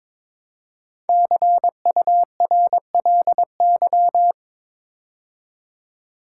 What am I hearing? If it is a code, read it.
Morse code, heard as a keyed tone, about 22 wpm, 700 Hz. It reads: CURLY
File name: practice / 22wpm / 5 Letter Words - First 3 Letters Given - 22wpm